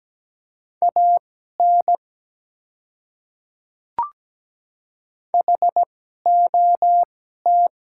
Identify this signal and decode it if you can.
Morse code, heard as a keyed tone, about 17 wpm, spaced as for 12 wpm, 700 Hz. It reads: AN HOT
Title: 2x Character Spacing - Top 100 Words - 17wpm